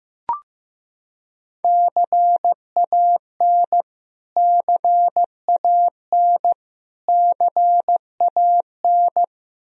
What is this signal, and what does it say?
Morse code, heard as a keyed tone, about 15 wpm, 700 Hz. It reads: CAN CAN CAN
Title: Top 100 Words Repeated 15wpm